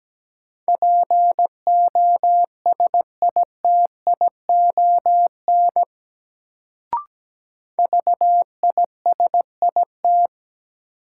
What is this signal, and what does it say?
Morse code, heard as a keyed tone, about 17 wpm, 700 Hz. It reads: POSITION VISIT
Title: Sentences from Top 900 Words - New Word Drill - 17wpm